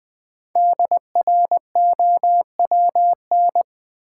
Morse code, heard as a keyed tone, about 20 wpm, 700 Hz. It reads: DROWN
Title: 5 Letter Words - First 3 Letters Given - 20wpm